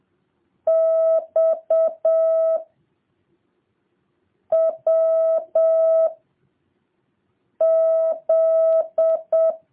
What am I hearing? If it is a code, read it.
Morse code, heard as a keyed tone, about 7 wpm, 650 Hz. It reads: X W Z